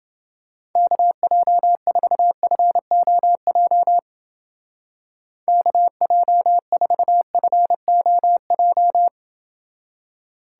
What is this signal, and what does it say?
Morse code, heard as a keyed tone, about 30 wpm, 700 Hz. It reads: KJ4FOJ KJ4FOJ